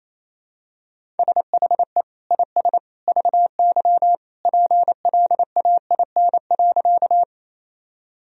Morse code, heard as a keyed tone, about 27 wpm, 700 Hz. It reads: SHE IS VY PLAIN.
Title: Sentences from Top 400 Words and Abbreviations 28wpm